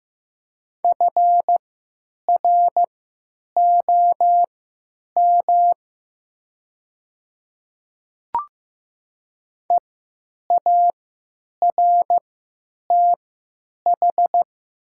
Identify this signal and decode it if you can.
Morse code, heard as a keyed tone, about 15 wpm, spaced as for 9 wpm, 700 Hz. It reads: FROM EARTH
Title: 3x Character Spacing - Top 300 Words - 15wpm